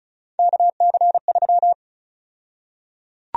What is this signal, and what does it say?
Morse code, heard as a keyed tone, about 34 wpm, 700 Hz. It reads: KC3
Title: USA Call Sign Prefixes - 35wpm